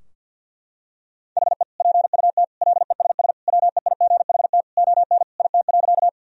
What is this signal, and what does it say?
Morse code, heard as a keyed tone, about 51 wpm, 700 Hz. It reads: HE PUT LESS WEIGHT ON IT.
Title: Sentences from Top 1000 Words - Review Part 3 - 50wpm